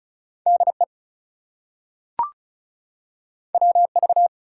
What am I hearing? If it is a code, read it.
Morse code, heard as a keyed tone, about 36 wpm, 700 Hz. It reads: DE WV
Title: US States and Canadian Provinces 35wpm